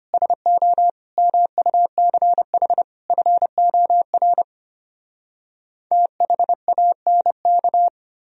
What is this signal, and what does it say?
Morse code, heard as a keyed tone, about 33 wpm, 700 Hz. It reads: SO MUCH FOR THANK